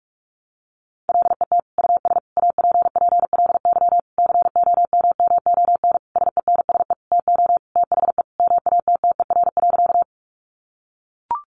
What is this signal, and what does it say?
Morse code, heard as a keyed tone, about 44 wpm, 700 Hz. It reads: LET US APPLY COMMON SENSE TO THE MATTER.
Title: Sentences from Top 1200 Words 45wpm